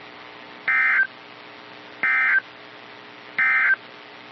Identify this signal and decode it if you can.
Emergency Alert System (SAME) tones, heard as harsh buzzing bursts. It is heard from 0.6 to 3.8 s.
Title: NOAA Weather Radio - KEC55